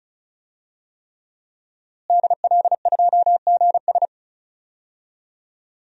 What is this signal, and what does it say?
Morse code, heard as a keyed tone, about 36 wpm, 700 Hz. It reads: DL2GS
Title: DXCC-100 Call Signs - 5 Characters - Repeated 3x - 35wpm